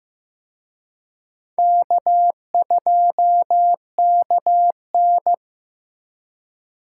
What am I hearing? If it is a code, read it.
Morse code, heard as a keyed tone, about 18 wpm, 700 Hz. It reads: K2KN
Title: Speed Racing - USA 1x2 Call Signs 15wpm